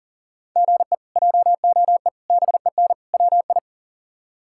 Morse code, heard as a keyed tone, about 40 wpm, 700 Hz. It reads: GE JOE BEN WI
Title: K1USN SST QSO Practice - 40wpm